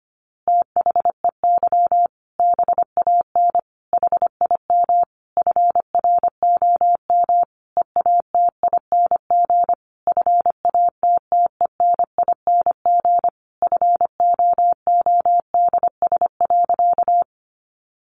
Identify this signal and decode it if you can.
Morse code, heard as a keyed tone, about 25 wpm, 700 Hz. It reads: THEY BAN HIM FROM EATING FATTENING FOODS.
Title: Easy Phrases 25wpm